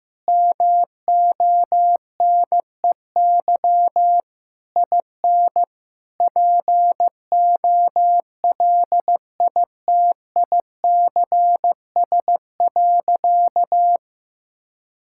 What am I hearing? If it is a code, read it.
Morse code, heard as a keyed tone, about 16 wpm, 700 Hz. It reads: MONEY IN POLITICS.
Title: Sentences from Top 1000 Words - Review Part 6 - 15wpm